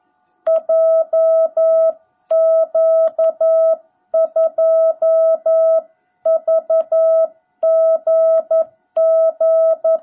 Morse code, heard as a keyed tone, about 11 wpm, 660 Hz. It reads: JQ2VGG
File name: jh8cbh / 2021cw / cw17b